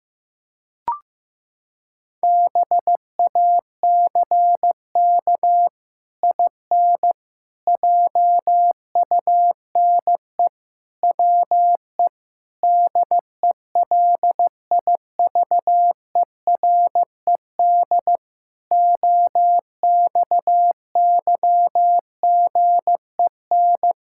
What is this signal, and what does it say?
Morse code, heard as a keyed tone, about 15 wpm, 700 Hz. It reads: BACK IN JUNE WE DELIVERED OXYGEN
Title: Pangram Challenge v2 - 15wpm